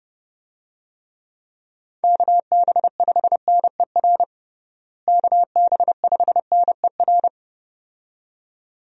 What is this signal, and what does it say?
Morse code, heard as a keyed tone, about 30 wpm, 700 Hz. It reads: KB5NER KB5NER
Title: USA Call Signs - 6 Characters - Repeated 2x - 30wpm